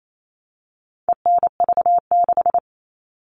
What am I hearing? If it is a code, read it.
Morse code, heard as a keyed tone, about 28 wpm, 700 Hz. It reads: ENV6
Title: All Cut Numbers - 4-Digit 28wpm